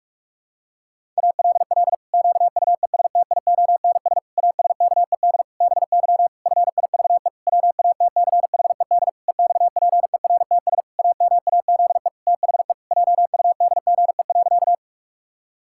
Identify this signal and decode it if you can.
Morse code, heard as a keyed tone, about 44 wpm, 700 Hz. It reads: ALL QUESTIONS ASKED BY FIVE WATCHED EXPERTS AMAZE THE JUDGE.
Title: Pangram Challenge v2 - 45wpm